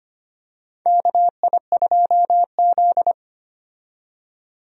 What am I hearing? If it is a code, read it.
Morse code, heard as a keyed tone, about 25 wpm, 700 Hz. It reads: KI2Z
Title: USA Call Signs - 4 Characters - 25wpm